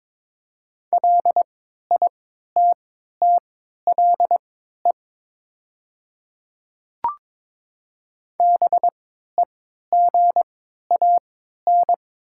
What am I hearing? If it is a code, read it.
Morse code, heard as a keyed tone, about 22 wpm, spaced as for 13 wpm, 700 Hz. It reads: LITTLE BEGAN
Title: 3x Character Spacing - Top 500 Words - 22wpm